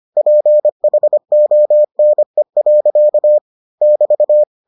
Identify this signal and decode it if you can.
Morse code, heard as a keyed tone, about 25 wpm, 600 Hz. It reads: PHONE. <BT>